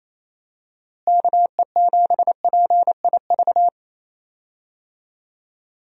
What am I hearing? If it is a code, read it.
Morse code, heard as a keyed tone, about 28 wpm, 700 Hz. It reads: KE7PIV
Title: USA Call Signs - 6 Characters - Repeated 3x - 28wpm